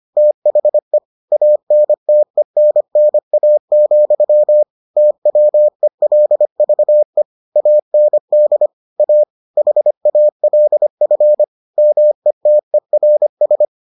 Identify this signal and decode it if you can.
Morse code, heard as a keyed tone, about 25 wpm, 600 Hz. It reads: THE ANTENNA, TWELVE AND A HALF METERS